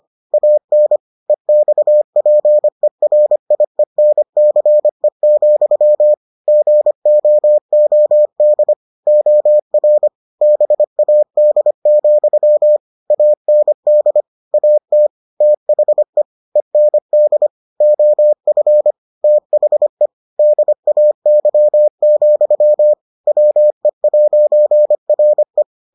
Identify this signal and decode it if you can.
Morse code, heard as a keyed tone, about 25 wpm, 600 Hz. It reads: AN EXPERIENCE, GOOD OR BAD, AND AT THE END OF THE DAY, WE'RE